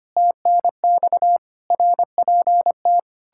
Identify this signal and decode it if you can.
Morse code, heard as a keyed tone, about 33 wpm, 700 Hz. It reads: TNX RPT